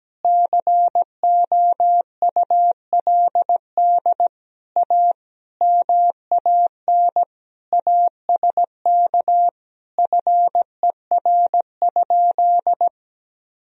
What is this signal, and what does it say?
Morse code, heard as a keyed tone, about 17 wpm, 700 Hz. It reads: COULD A MAN ASK FER?
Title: Sentences from Top 200 Words and Abbreviations 17wpm